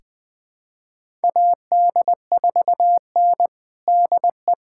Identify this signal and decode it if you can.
Morse code, heard as a keyed tone, about 21 wpm, 700 Hz. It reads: AD4N DE
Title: USA Callsign DE Callsign - 4 Characters - 20wpm